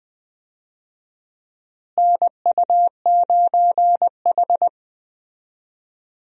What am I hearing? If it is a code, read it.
Morse code, heard as a keyed tone, about 19 wpm, 700 Hz. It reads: NU9H
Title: USA Call Signs - 4 Characters - Repeated 3x - 20wpm